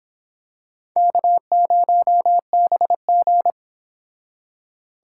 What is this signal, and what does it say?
Morse code, heard as a keyed tone, about 26 wpm, 700 Hz. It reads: K0BG